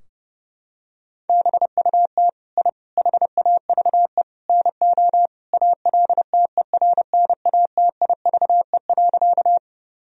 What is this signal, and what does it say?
Morse code, heard as a keyed tone, about 30 wpm, 700 Hz. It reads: BUT I HAVE NO ALTERNATIVE.
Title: Sentences from Top 1000 Words - Review Part 4 - 30wpm